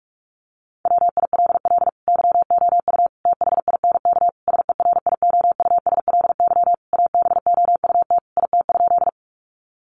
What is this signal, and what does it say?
Morse code, heard as a keyed tone, about 44 wpm, 700 Hz. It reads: WILL YOU THINK SERIOUSLY ABOUT IT?
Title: Sentences from Top 1200 Words 45wpm